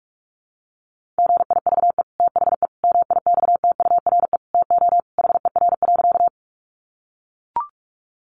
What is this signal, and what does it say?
Morse code, heard as a keyed tone, about 44 wpm, 700 Hz. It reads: GIVE THE MIXTURE TO HER.
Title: Sentences from Top 1500 Words 45wpm